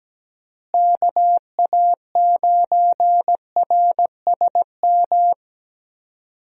Morse code, heard as a keyed tone, about 17 wpm, 700 Hz. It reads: KA9RSM
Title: USA Call Signs - 6 Characters - Repeated 3x - 17wpm